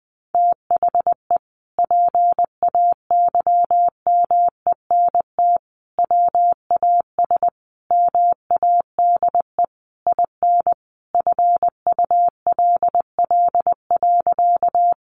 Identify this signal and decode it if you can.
Morse code, heard as a keyed tone, about 20 wpm, 700 Hz. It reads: THE PAYMENT WAS MADE IN FULL.
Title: Sentences from Top 1200 Words 20wpm